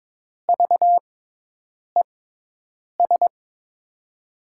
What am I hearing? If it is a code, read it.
Morse code, heard as a keyed tone, about 22 wpm, spaced as for 8 wpm, 700 Hz. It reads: VES
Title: USA Call Signs - 5 Characters - 6x Character Spacing - 22wpm